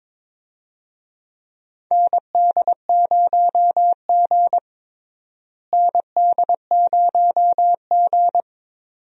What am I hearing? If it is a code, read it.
Morse code, heard as a keyed tone, about 22 wpm, 700 Hz. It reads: ND0G ND0G
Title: USA Call Signs - 4 Characters - Repeated 2x - 22wpm